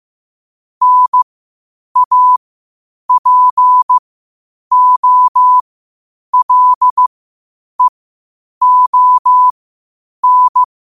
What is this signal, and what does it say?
Morse code, heard as a keyed tone, about 15 wpm, spaced as for 8 wpm, 1000 Hz. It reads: NAPOLEON
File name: qotdfiles / 8wpm